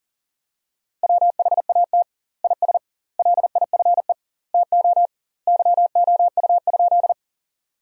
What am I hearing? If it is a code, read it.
Morse code, heard as a keyed tone, about 40 wpm, 700 Hz. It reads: WHAT IS LIFE TO YOU?
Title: 2x Word-Spacing - Sentences from Top 300 Words - 40wpm